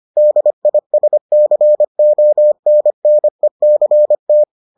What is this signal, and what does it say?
Morse code, heard as a keyed tone, about 25 wpm, 600 Hz. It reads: DISCONNECT